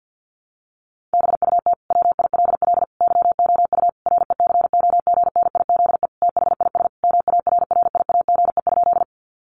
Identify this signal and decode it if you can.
Morse code, heard as a keyed tone, about 50 wpm, 700 Hz. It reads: BUT WILL YOU RECOGNIZE THIS MARRIAGE?
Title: Sentences from Top 1400 Words 50wpm